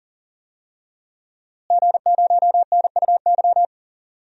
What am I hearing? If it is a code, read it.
Morse code, heard as a keyed tone, about 38 wpm, 700 Hz. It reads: G0NUY
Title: DXCC-100 Call Signs - 5 Characters - Repeated 2x - 40wpm